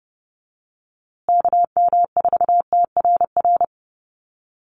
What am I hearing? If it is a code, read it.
Morse code, heard as a keyed tone, about 30 wpm, 700 Hz. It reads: KM4TRR
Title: USA Call Signs Repeated 2x 30wpm